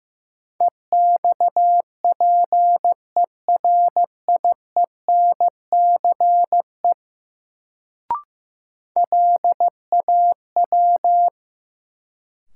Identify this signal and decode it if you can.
Morse code, heard as a keyed tone, about 15 wpm, 700 Hz. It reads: EXPERIENCE LAW
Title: Sentences from Top 600 Words - New Word Drill - 15wpm